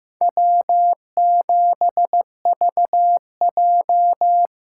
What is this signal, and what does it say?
Morse code, heard as a keyed tone, about 18 wpm, 700 Hz. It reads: W7VJ
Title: Speed Racing - USA Call Signs 15wpm